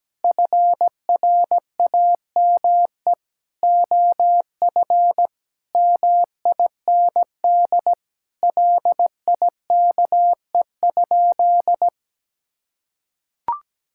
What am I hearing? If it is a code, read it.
Morse code, heard as a keyed tone, about 17 wpm, 700 Hz. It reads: FRAME OF MIND LIKE?
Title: Sentences from Top 1000 Words - Review Part 6 - 17wpm